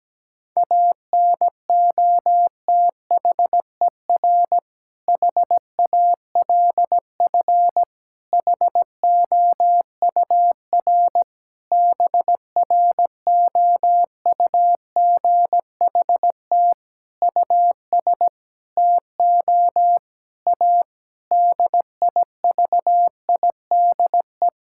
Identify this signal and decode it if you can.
Morse code, heard as a keyed tone, about 17 wpm, 700 Hz. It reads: ANOTHER HALF HOUR BROUGHT US TO A DIVIDE
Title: Sentences from Top 1600 Words 17wpm